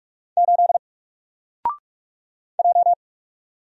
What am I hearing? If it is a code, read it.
Morse code, heard as a keyed tone, about 44 wpm, 700 Hz. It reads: 8 J
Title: Single Letter-Number Alt 45wpm